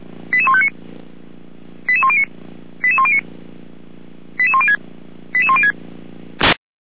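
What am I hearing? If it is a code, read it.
Five-tone selective call (ZVEI), heard as quick runs of five tones, tones of about 80 ms. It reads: 88159 88109 88109 88106 88106
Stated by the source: ZVEI five-tone call 88402 received by SDR at feuerwehr-orlamuende.org fire brigade.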